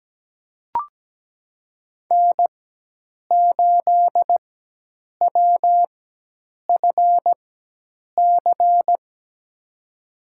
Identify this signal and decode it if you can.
Morse code, heard as a keyed tone, about 17 wpm, spaced as for 8 wpm, 700 Hz. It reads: N8WFC
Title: USA Call Signs - 5 Characters - 4x Character Spacing - 17wpm